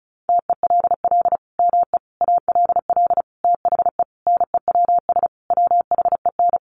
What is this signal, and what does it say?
Morse code, heard as a keyed tone, about 34 wpm, 700 Hz. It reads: TELL ME ALL THE NEWS WHEN
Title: Sentences from Top 800 Words 35wpm